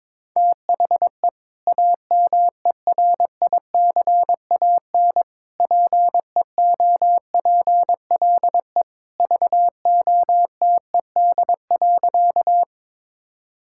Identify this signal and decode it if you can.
Morse code, heard as a keyed tone, about 22 wpm, 700 Hz. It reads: THE AMERICAN PEOPLE VOTED.
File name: practice / 22wpm / Sentences with 3-Word Phrases - 22wpm